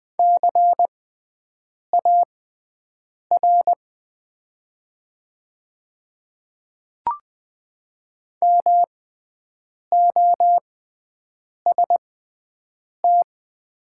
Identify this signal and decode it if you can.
Morse code, heard as a keyed tone, about 21 wpm, spaced as for 7 wpm, 700 Hz. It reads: CAR MOST
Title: 6x Character Spacing - Top 500 Words - 20wpm